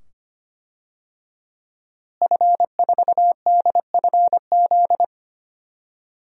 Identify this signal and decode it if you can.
Morse code, heard as a keyed tone, about 25 wpm, 700 Hz. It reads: F4DFZ
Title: DXCC-100 Call Signs - 5 Characters - Repeated 2x - 25wpm